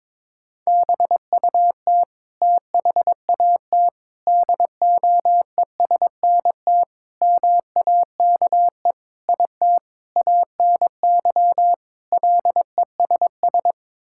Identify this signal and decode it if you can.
Morse code, heard as a keyed tone, about 22 wpm, 700 Hz. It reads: BUT THAT DOESNT MAKE IT ANY LESS
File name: practice / 22wpm / Sets of 7 Words - 22wpm